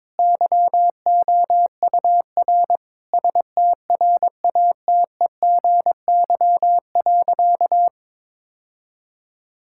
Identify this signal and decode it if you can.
Morse code, heard as a keyed tone, about 22 wpm, 700 Hz. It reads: YOUR STRATEGY.
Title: Sentences from Top 1000 Words - Review Part 5 - 22wpm